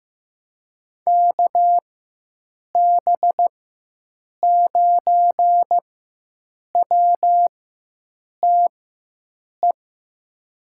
Morse code, heard as a keyed tone, about 15 wpm, spaced as for 7 wpm, 700 Hz. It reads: KB9WTE
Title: USA Call Signs - 6 Characters - 4x Character Spacing - 15wpm